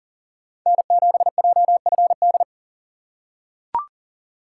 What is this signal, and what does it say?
Morse code, heard as a keyed tone, about 40 wpm, 700 Hz. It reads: N7JFD
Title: USA Call Signs - 5 Characters - 6x Character Spacing - 40wpm